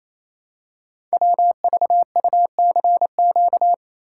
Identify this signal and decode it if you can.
Morse code, heard as a keyed tone, about 28 wpm, 700 Hz. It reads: WVUCQ
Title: Code Groups - 5 Letters - 28wpm